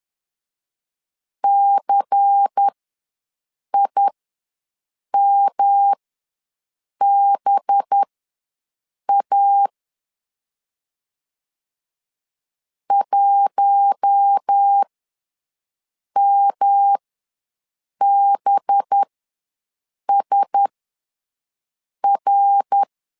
Morse code, heard as a keyed tone, about 10 wpm, spaced as for 6 wpm, 800 Hz. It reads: CIMBA 1MBSR
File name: LEZIONI / 5 LBVC / 5 LBVC ws 10wpm 3733